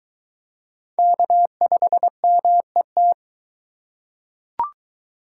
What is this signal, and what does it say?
Morse code, heard as a keyed tone, about 20 wpm, 700 Hz. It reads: K5MET